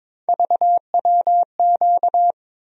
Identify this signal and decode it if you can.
Morse code, heard as a keyed tone, about 22 wpm, 700 Hz. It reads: VWQ
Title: USA Call Sign Postfixes - 3 Letters 22wpm